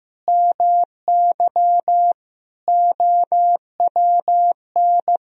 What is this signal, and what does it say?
Morse code, heard as a keyed tone, about 15 wpm, 700 Hz. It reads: MY OWN
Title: Sentences from Top 1000 Words - Review Part 4 - 15wpm